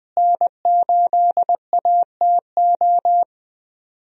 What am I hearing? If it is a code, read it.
Morse code, heard as a keyed tone, about 20 wpm, 700 Hz. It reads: N8ATO
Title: USA Call Signs - 5 Characters - Repeated 3x - 20wpm